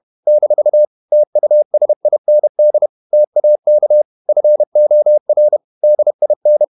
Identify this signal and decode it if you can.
Morse code, heard as a keyed tone, about 31 wpm, 600 Hz. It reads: - TUSIND TAK FOR DIN
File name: Ferie QRQ 31wpm